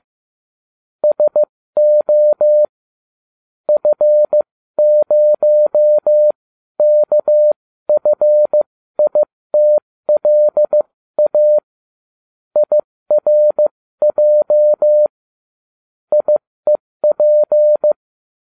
Morse code, heard as a keyed tone, about 15 wpm, spaced as for 13 wpm, 600 Hz. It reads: SO F0KFITLA IRJ IEP